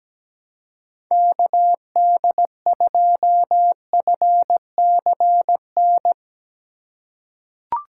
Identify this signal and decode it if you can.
Morse code, heard as a keyed tone, about 17 wpm, 700 Hz. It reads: KD2FCN
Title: USA Call Signs - 6 Characters - 6x Character Spacing - 17wpm